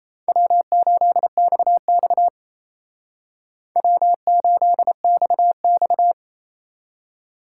Morse code, heard as a keyed tone, about 34 wpm, 700 Hz. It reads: W8XX W8XX
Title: Speed Racing - USA Call Signs 28wpm